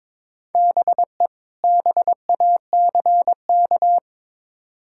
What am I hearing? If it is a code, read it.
Morse code, heard as a keyed tone, about 22 wpm, 700 Hz. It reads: BE BACK
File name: practice / 22wpm / Build A Phrase - 5-Word Phrases - 22wpm